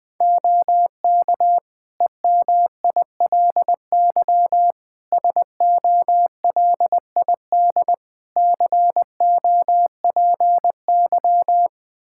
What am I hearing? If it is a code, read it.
Morse code, heard as a keyed tone, about 20 wpm, 700 Hz. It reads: OK EMILY SOLID COPY
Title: Practice QSO - Exchange 3 - More Info - 20wpm